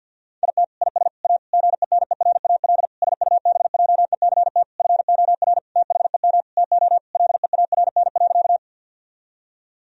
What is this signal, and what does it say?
Morse code, heard as a keyed tone, about 50 wpm, 700 Hz. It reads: IT IS A GENERAL SUBJECT FOR THEM TO LEARN.
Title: Sentences from Top 1000 Words - Review Part 3 - 50wpm